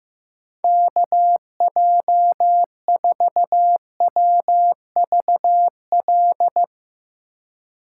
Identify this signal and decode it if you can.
Morse code, heard as a keyed tone, about 15 wpm, 700 Hz. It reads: KJ4WVL
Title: USA Call Signs - 6 Characters - 3x Character Spacing - 15wpm